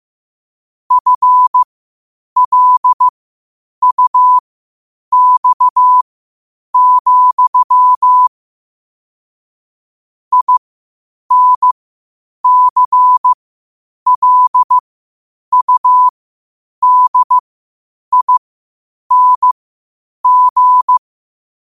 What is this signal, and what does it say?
Morse code, heard as a keyed tone, about 15 wpm, spaced as for 8 wpm, 1000 Hz. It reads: FLUX, INCLUDING
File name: qotdfiles / 8wpm